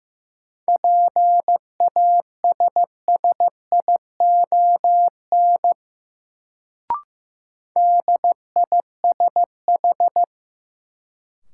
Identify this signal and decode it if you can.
Morse code, heard as a keyed tone, about 15 wpm, 700 Hz. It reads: PASSION DISH
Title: Sentences from Top 1300 Words - New Word Drill - 15wpm